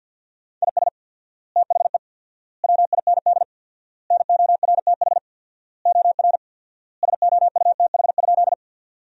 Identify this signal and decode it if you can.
Morse code, heard as a keyed tone, about 50 wpm, 700 Hz. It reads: IS THE WIND NORTH OR SOUTH?
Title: 4x Word-Spacing - Sentences from Top 400 Words - 50wpm